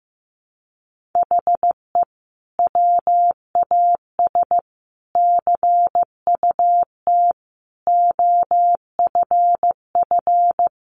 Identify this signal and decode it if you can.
Morse code, heard as a keyed tone, about 15 wpm, 700 Hz. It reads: HE WAS CUT OFF
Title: Sentences from Top 1100 Words 15wpm